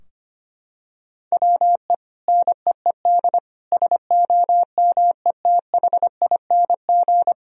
Morse code, heard as a keyed tone, about 25 wpm, 700 Hz. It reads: WE NEED SOMETHING
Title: Sentences from Top 1000 Words - Review Part 4 - 25wpm